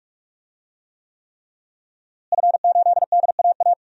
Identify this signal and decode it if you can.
Morse code, heard as a keyed tone, about 46 wpm, 700 Hz. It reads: F8DAA
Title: DXCC-100 Call Signs - 5 Characters - Repeated 3x - 45wpm